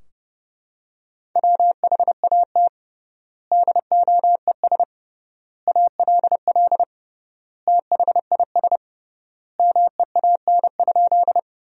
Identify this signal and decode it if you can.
Morse code, heard as a keyed tone, about 30 wpm, 700 Hz. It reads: WHAT DOES ALL THIS MEAN?
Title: 3x Word-Spacing - Sentences from Top 200 Words - 30wpm